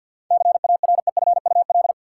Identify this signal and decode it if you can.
Morse code, heard as a keyed tone, about 52 wpm, 700 Hz. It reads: CAREFUL
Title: Sentences from Top 1600 Words - New Word Drill Alt - 50wpm